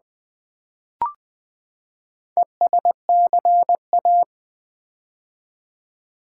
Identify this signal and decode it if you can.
Morse code, heard as a keyed tone, about 20 wpm, 700 Hz. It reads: ESCA
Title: Build A Word - 4 Letters At A Time - 8-Letter Words - 20wpm